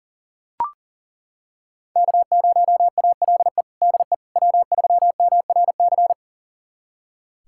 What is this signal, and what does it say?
Morse code, heard as a keyed tone, about 40 wpm, 700 Hz. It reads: K0ALE DE W3MRC
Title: USA Callsign DE Callsign - 5 Characters - 40wpm